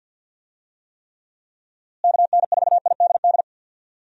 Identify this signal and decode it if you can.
Morse code, heard as a keyed tone, about 51 wpm, 700 Hz. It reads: KN4IDD
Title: USA Call Signs - 6 Characters - Repeated 3x - 50wpm